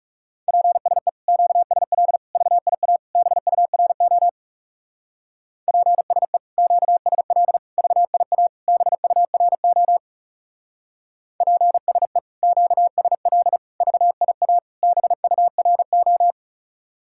Morse code, heard as a keyed tone, about 34 wpm, 700 Hz. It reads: PSE QSL VIA BURO PSE QSL VIA BURO PSE QSL VIA BURO
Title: Speed-Racing Common QSO Phrases Encore - 30wpm